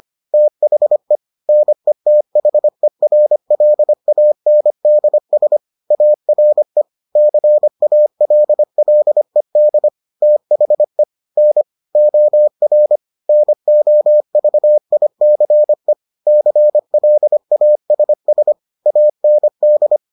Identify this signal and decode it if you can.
Morse code, heard as a keyed tone, about 25 wpm, 600 Hz. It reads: THE NETHERLANDS ARE CALLED THE N OR NOVICE CLASS AND